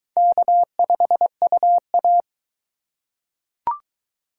Morse code, heard as a keyed tone, about 22 wpm, 700 Hz. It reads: K5UA